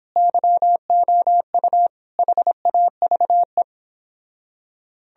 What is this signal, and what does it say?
Morse code, heard as a keyed tone, about 30 wpm, 700 Hz. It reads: YOU HAVE